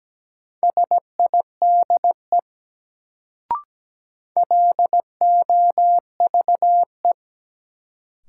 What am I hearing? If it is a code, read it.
Morse code, heard as a keyed tone, about 17 wpm, 700 Hz. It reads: SIDE LOVE
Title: Sentences from Top 400 Words - New Word Drill - 17wpm